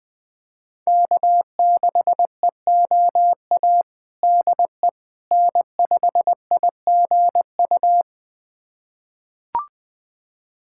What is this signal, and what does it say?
Morse code, heard as a keyed tone, about 20 wpm, 700 Hz. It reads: K6EOA DE N5IGU
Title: USA Callsign DE Callsign - 5 Characters - 20wpm